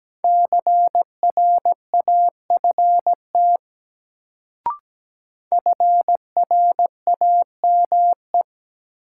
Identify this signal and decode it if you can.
Morse code, heard as a keyed tone, about 17 wpm, 700 Hz. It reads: CRAFT FRAME
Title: Sentences from Top 1000 Words - Review Part 6 - New Word Drill - 17wpm